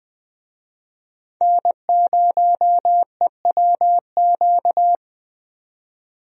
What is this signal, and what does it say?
Morse code, heard as a keyed tone, about 20 wpm, 700 Hz. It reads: N0EWQ
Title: USA Call Signs - 5 Characters - 20wpm